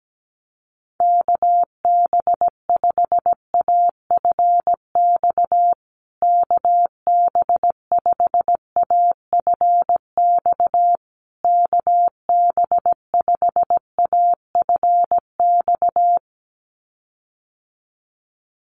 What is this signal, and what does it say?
Morse code, heard as a keyed tone, about 17 wpm, 700 Hz. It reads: KB5AFX KB5AFX KB5AFX
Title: USA Call Signs Repeated 3x 17wpm